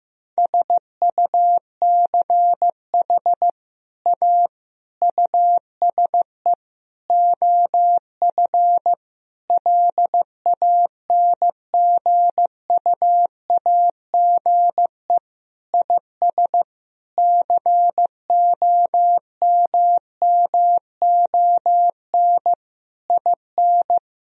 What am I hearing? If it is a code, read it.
Morse code, heard as a keyed tone, about 15 wpm, 700 Hz. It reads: SUCH A USE OF LANGUAGE IS COMMON IN
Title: Sentences from Top 1000 Words - Review Part 3 - 15wpm